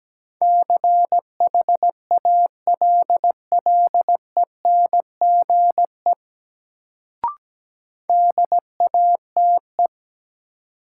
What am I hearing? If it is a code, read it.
Morse code, heard as a keyed tone, about 17 wpm, 700 Hz. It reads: CHALLENGE DATE
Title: Sentences from Top 1000 Words - Review Part 7 - New Word Drill - 17wpm